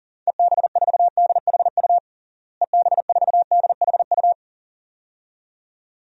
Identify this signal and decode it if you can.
Morse code, heard as a keyed tone, about 37 wpm, 700 Hz. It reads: EB4DHU EB4DHU
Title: DXCC-100 Call Signs - 6 Characters - Repeated 2x - 40wpm